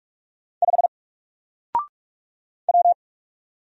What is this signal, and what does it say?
Morse code, heard as a keyed tone, about 46 wpm, 700 Hz. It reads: 5 W
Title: Single Letter-Number 45wpm